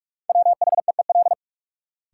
Morse code, heard as a keyed tone, about 46 wpm, 700 Hz. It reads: WHEEL
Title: Sentences from Top 500 Words - New Word Drill - 45wpm